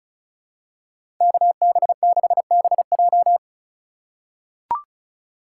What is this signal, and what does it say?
Morse code, heard as a keyed tone, about 34 wpm, 700 Hz. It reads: KB6BJ
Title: USA Call Signs - 4x Character Spacing - 35wpm